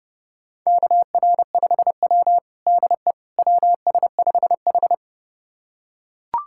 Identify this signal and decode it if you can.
Morse code, heard as a keyed tone, about 30 wpm, 700 Hz. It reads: KR5W DE WS5H
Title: USA Callsign DE Callsign - 4 Characters - 30wpm